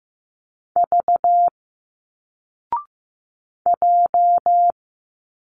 Morse code, heard as a keyed tone, about 15 wpm, 700 Hz. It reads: V J
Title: Single Letters 15wpm